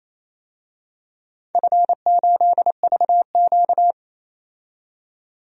Morse code, heard as a keyed tone, about 28 wpm, 700 Hz. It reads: F8VQ
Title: DXCC-100 Call Signs - 4 Characters - 28wpm